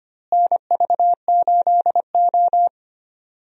Morse code, heard as a keyed tone, about 25 wpm, 700 Hz. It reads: NV8O
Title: USA Call Signs - 4 Characters - Repeated 3x - 25wpm